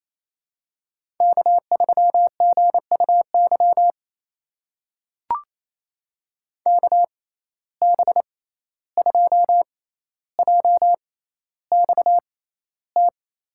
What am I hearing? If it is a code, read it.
Morse code, heard as a keyed tone, about 28 wpm, spaced as for 14 wpm, 700 Hz. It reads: K3GUY KB2JXT
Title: USA Call Signs - 6x Character Spacing - 28wpm